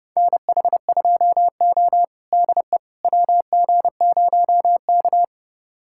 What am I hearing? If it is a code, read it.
Morse code, heard as a keyed tone, about 30 wpm, 700 Hz. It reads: NH2O DE WG0K
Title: USA Callsign DE Callsign - 4 Characters - 30wpm